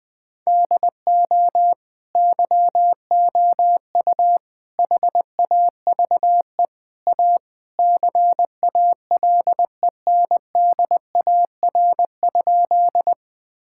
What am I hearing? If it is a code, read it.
Morse code, heard as a keyed tone, about 20 wpm, 700 Hz. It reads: DO YOU HAVE A CALENDAR?
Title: Sentences from Top 1600 Words 20wpm